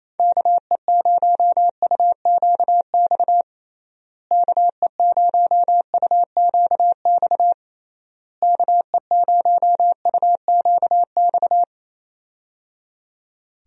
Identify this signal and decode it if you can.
Morse code, heard as a keyed tone, about 27 wpm, 700 Hz. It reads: KE0UQX KE0UQX KE0UQX
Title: USA Call Signs - 6 Characters - Repeated 3x - 28wpm